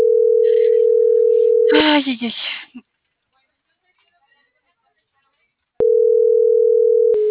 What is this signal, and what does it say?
Ringback tone (North America), 440 + 480 Hz, with speech.
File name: CENTRAL AGUILERA